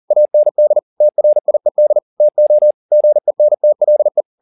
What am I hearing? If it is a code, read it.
Morse code, heard as a keyed tone, about 40 wpm, 600 Hz. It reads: AND TRIED TO GENTLE